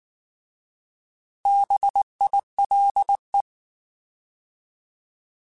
Morse code, heard as a keyed tone, about 19 wpm, 800 Hz. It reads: BILE
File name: Alphabetical / Alphabetical mp3s / Bs @18